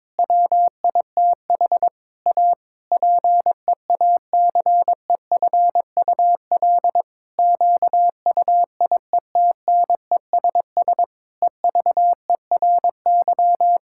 Morse code, heard as a keyed tone, about 22 wpm, 700 Hz. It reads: WITH A PEACEFUL QUIETNESS EVERY
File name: Difficult Sentences with Hints - 22wpm